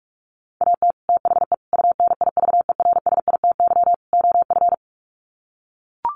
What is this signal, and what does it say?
Morse code, heard as a keyed tone, about 44 wpm, 700 Hz. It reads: AT THE UNIVERSITY OF
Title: Sets of 4 Words 45wpm